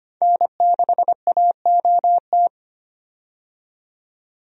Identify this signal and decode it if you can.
Morse code, heard as a keyed tone, about 25 wpm, 690 Hz. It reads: N6AOT
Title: USA Call Signs - 5 Characters - 25wpm